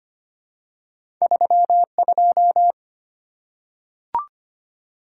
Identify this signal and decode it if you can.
Morse code, heard as a keyed tone, about 25 wpm, 700 Hz. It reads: 32
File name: Two Letter-Number Combinations - 25wpm